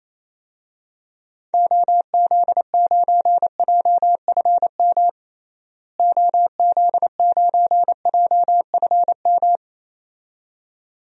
Morse code, heard as a keyed tone, about 28 wpm, 700 Hz. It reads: OZ9JFM OZ9JFM
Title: DXCC-100 Call Signs - 6 Characters - Repeated 2x - 28wpm